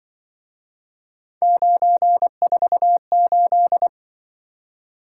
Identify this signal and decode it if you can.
Morse code, heard as a keyed tone, about 25 wpm, 700 Hz. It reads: 948
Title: Speed-Racing - Three Letter-Number Combinations - 20wpm